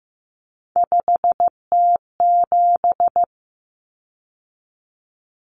Morse code, heard as a keyed tone, about 15 wpm, 700 Hz. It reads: 5T7
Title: Common Cut Numbers - 3-Digit 15wpm